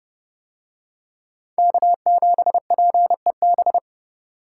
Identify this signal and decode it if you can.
Morse code, heard as a keyed tone, about 29 wpm, 700 Hz. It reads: K7PEB